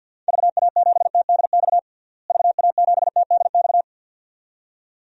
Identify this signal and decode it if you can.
Morse code, heard as a keyed tone, about 48 wpm, 700 Hz. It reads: VA7TDX VA7TDX
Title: DXCC-100 Call Signs - 6 Characters - Repeated 2x - 50wpm